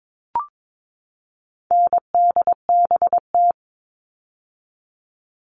Morse code, heard as a keyed tone, about 22 wpm, 700 Hz. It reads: NDBT
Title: All Cut Numbers - 4-Digit 22wpm